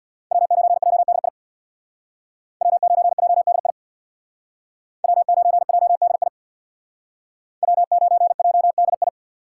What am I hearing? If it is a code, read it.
Morse code, heard as a keyed tone, about 63 wpm, 700 Hz. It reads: W9JDI W9JDI W9JDI W9JDI